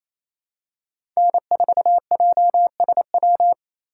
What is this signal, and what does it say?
Morse code, heard as a keyed tone, about 28 wpm, 700 Hz. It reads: N4JSW
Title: USA Call Signs - 5 Characters - 6x Character Spacing - 28wpm